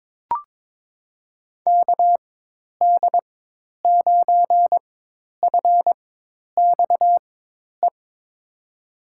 Morse code, heard as a keyed tone, about 21 wpm, spaced as for 10 wpm, 700 Hz. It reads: KD9FXE
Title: USA Call Signs - 4x Character Spacing - 22wpm